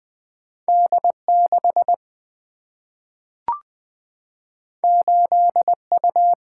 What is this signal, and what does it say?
Morse code, heard as a keyed tone, about 20 wpm, 700 Hz. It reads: D6 8U
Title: Single-Letter Single-Number 20wpm